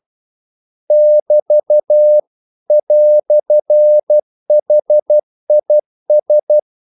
Morse code, heard as a keyed tone, about 12 wpm, 600 Hz. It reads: <BT> "HIS
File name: CW / Practice / EnglishQuotations0112